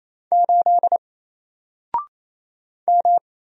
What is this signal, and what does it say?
Morse code, heard as a keyed tone, about 28 wpm, 700 Hz. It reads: 8 M
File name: Single-Letter-Number-28wpm